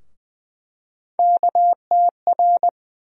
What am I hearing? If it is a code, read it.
Morse code, heard as a keyed tone, about 21 wpm, 700 Hz. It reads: KTR
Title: USA Call Sign Postfixes - 3 Letters 20wpm